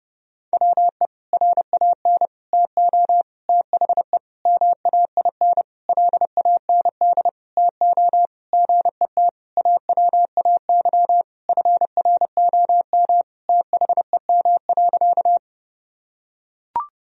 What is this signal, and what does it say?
Morse code, heard as a keyed tone, about 30 wpm, 700 Hz. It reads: WE RAN TO THE MAIN LAND TO GET AWAY FROM THEM.
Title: Sentences from Top 1000 Words - Review Part 2 - 30wpm